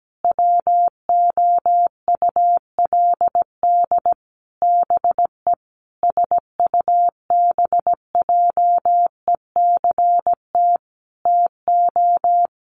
Morse code, heard as a keyed tone, about 17 wpm, 700 Hz. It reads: WOULD BE SUBJECT TO
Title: Sentences from Top 1300 Words 17wpm